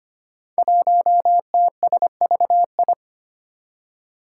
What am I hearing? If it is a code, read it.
Morse code, heard as a keyed tone, about 25 wpm, 700 Hz. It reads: 1TSVI
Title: Five Challenging Letter-Number Combinations 25wpm